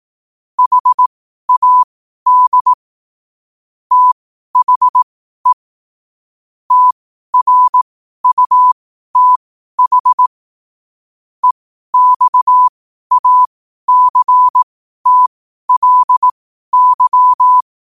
Morse code, heard as a keyed tone, about 18 wpm, spaced as for 12 wpm, 1000 Hz. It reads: HAD THE TRUTH EXACTLY
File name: qotdfiles / 12wpm